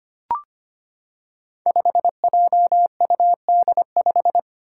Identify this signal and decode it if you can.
Morse code, heard as a keyed tone, about 25 wpm, 700 Hz. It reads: 5JUD5
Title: Five Challenging Letter-Number Combinations 25wpm